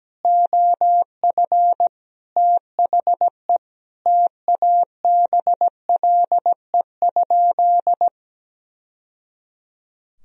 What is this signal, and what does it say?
Morse code, heard as a keyed tone, about 17 wpm, 700 Hz. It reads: OF THE TABLE?
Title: Sentences from Top 1000 Words - Review Part 3 - 17wpm